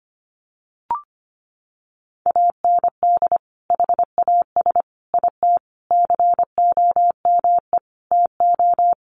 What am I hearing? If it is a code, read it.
Morse code, heard as a keyed tone, about 25 wpm, 700 Hz. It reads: AND HAS IT COME TO
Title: Sentences from Top 100 Words 25wpm